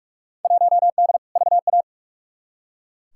Morse code, heard as a keyed tone, about 42 wpm, 700 Hz. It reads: 1D VA
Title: Field Day Exchanges - 45wpm